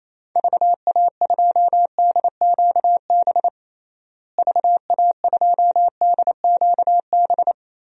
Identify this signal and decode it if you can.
Morse code, heard as a keyed tone, about 28 wpm, 700 Hz. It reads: VA2DQB VA2DQB
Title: DXCC-100 Call Signs - 6 Characters - Repeated 3x - 28wpm